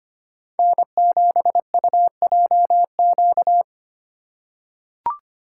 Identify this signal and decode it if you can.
Morse code, heard as a keyed tone, about 25 wpm, 700 Hz. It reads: N7UJQ
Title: USA Call Signs - 5 Characters - 4x Character Spacing - 25wpm